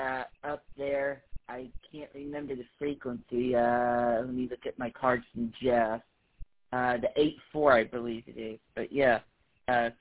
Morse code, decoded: none

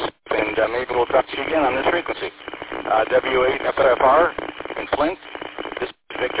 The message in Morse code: none

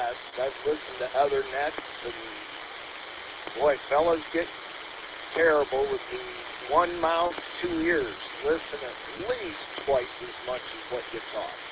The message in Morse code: none

